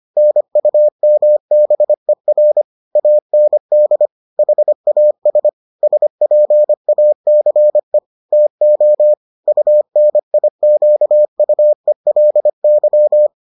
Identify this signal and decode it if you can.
Morse code, heard as a keyed tone, about 25 wpm, 600 Hz. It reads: NUMBER AND HAS SPACE TO UNIQUELY